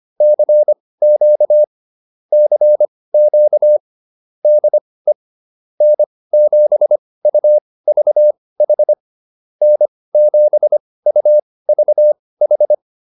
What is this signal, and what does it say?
Morse code, heard as a keyed tone, about 25 wpm, spaced as for 18 wpm, 600 Hz. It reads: CQ CQ DE N7UVH N7UVH
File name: QSO 105 18